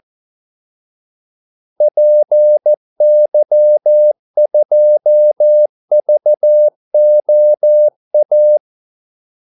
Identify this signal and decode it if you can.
Morse code, heard as a keyed tone, about 14 wpm, 600 Hz. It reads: PY2VOA
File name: Callsigns-14wpm